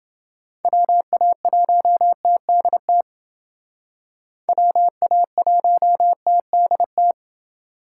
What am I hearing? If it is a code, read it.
Morse code, heard as a keyed tone, about 29 wpm, 700 Hz. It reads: WA1TDT WA1TDT